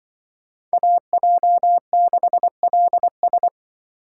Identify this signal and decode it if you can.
Morse code, heard as a keyed tone, about 24 wpm, 700 Hz. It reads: AJ6LS
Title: Speed Racing - USA 2x2 Call Signs 20wpm